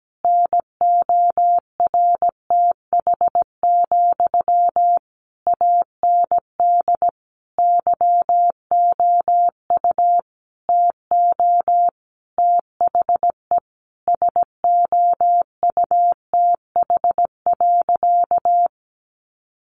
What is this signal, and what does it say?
Morse code, heard as a keyed tone, about 17 wpm, 700 Hz. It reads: NORTH, AND YOU TO THE SOUTH.
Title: Sentences from Top 400 Words 17wpm